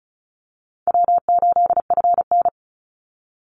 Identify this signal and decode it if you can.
Morse code, heard as a keyed tone, about 34 wpm, 700 Hz. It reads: W8FN
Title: Call Signs Repeated 2x - 35wpm